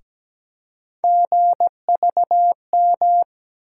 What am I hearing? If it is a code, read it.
Morse code, heard as a keyed tone, about 17 wpm, 700 Hz. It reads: GVM
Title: USA Call Sign Postfixes - 3 Letters 17wpm